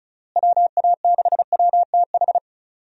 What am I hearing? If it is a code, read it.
Morse code, heard as a keyed tone, about 34 wpm, 700 Hz. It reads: WA6WTH
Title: USA Call Signs - 6 Characters - Repeated 3x - 35wpm